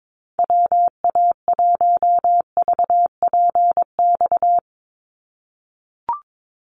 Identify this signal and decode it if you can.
Morse code, heard as a keyed tone, about 22 wpm, 700 Hz. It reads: WA1VPX
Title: USA Call Signs 22wpm